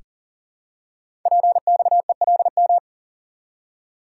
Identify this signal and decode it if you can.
Morse code, heard as a keyed tone, about 42 wpm, 690 Hz. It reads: PXELM
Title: Code Groups - 5 Letters - 40wpm